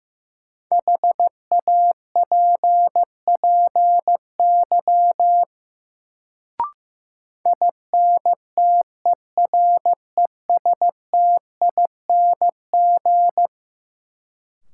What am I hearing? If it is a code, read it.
Morse code, heard as a keyed tone, about 16 wpm, 700 Hz. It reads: HAPPY INTERESTING
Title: Sentences from Top 1000 Words - New Word Drill - 15wpm